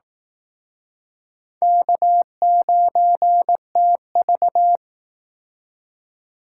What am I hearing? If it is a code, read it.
Morse code, heard as a keyed tone, about 18 wpm, 700 Hz. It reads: K9TV
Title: Speed Racing - USA 1x2 Call Signs 15wpm